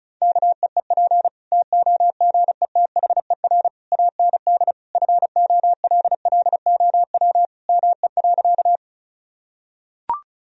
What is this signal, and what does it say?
Morse code, heard as a keyed tone, about 36 wpm, 700 Hz. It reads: KEEP TOGETHER AND FOLLOW ME.
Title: Sentences from Top 1000 Words - Review Part 1 - 35wpm